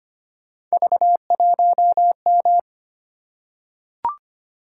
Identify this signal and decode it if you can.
Morse code, heard as a keyed tone, about 25 wpm, 700 Hz. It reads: V1M
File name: Three Letter-Number Combinations - 25wpm